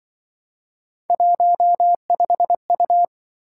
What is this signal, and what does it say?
Morse code, heard as a keyed tone, about 24 wpm, 700 Hz. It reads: 15U
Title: Speed-Racing - Three Letter-Number Combinations - 20wpm